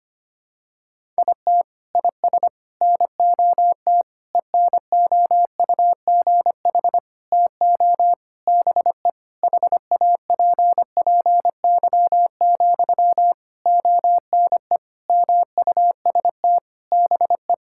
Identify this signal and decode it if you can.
Morse code, heard as a keyed tone, about 25 wpm, 700 Hz. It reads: IT IS NOT ENOUGH TO BE HAPPY, ONE MUST BE
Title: Sentences from Top 1000 Words - Review Part 6 - 25wpm